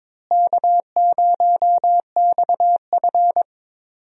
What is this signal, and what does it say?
Morse code, heard as a keyed tone, about 22 wpm, 700 Hz. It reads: K0XF
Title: USA Call Signs - 4 Characters - Repeated 2x - 22wpm